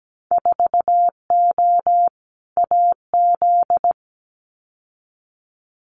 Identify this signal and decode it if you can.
Morse code, heard as a keyed tone, about 17 wpm, 700 Hz. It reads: 4O AZ
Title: Winter Field Day Exchanges Encore 17wpm